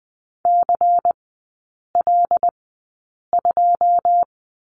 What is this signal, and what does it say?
Morse code, heard as a keyed tone, about 20 wpm, spaced as for 8 wpm, 700 Hz. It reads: CL2
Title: 145 Lesson 24e - Letter y - Three Characters 20wpm